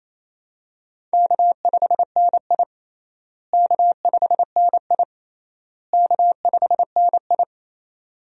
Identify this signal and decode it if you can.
Morse code, heard as a keyed tone, about 28 wpm, 700 Hz. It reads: K5NI K5NI K5NI
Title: USA Call Signs - 4 Characters - Repeated 3x - 28wpm